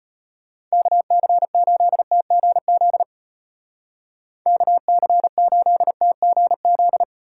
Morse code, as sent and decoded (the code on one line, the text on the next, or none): -.- -.-. ---.. - --. --.. / -.- -.-. ---.. - --. --..
KC8TGZ KC8TGZ